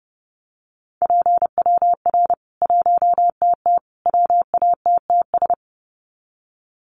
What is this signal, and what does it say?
Morse code, heard as a keyed tone, about 30 wpm, 700 Hz. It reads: PWR 1TT WATTS
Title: Common QSO Phrases 30wpm